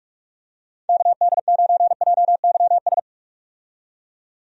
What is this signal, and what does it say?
Morse code, heard as a keyed tone, about 46 wpm, 700 Hz. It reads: KD9JYS
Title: USA Call Signs - 6 Characters - 45wpm